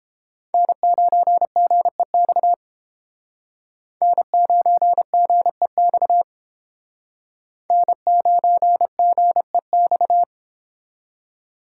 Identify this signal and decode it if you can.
Morse code, heard as a keyed tone, about 30 wpm, 700 Hz. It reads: N9GEX N9GEX N9GEX